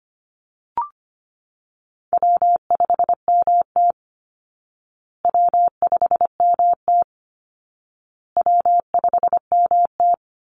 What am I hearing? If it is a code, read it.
Morse code, heard as a keyed tone, about 25 wpm, 700 Hz. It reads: W5MT W5MT W5MT
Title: Easy Call Signs Repeated 3x - 25wpm